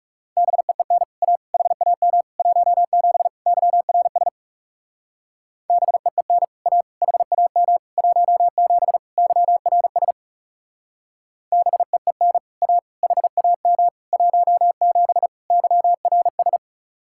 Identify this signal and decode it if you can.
Morse code, heard as a keyed tone, about 38 wpm, 700 Hz. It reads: BEEN A HAM 17 YRS BEEN A HAM 17 YRS BEEN A HAM 17 YRS